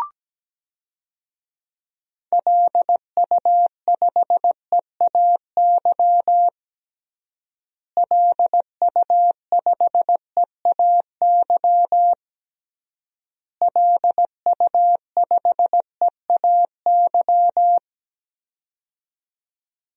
Morse code, heard as a keyed tone, about 17 wpm, 700 Hz. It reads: LU5EAY LU5EAY LU5EAY
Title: DXCC-100 Call Signs - 6 Characters - Repeated 3x - 17wpm